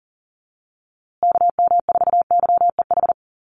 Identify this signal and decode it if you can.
Morse code, heard as a keyed tone, about 40 wpm, 700 Hz. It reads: KM4YEH
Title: USA Call Signs Repeated 2x 40wpm